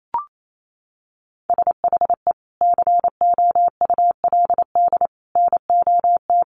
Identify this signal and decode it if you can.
Morse code, heard as a keyed tone, about 28 wpm, 700 Hz. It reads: SHE COULD NOT
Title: Sentences from Top 1100 Words 28wpm